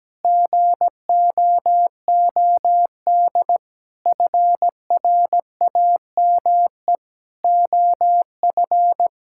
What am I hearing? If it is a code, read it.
Morse code, heard as a keyed tone, about 17 wpm, 700 Hz. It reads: GOOD FRAME OF
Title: Sentences from Top 1000 Words - Review Part 6 - 17wpm